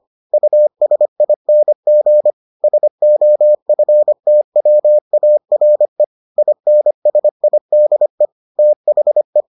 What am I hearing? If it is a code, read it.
Morse code, heard as a keyed tone, about 25 wpm, 600 Hz. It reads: USING SOFTWARE INSIDE THE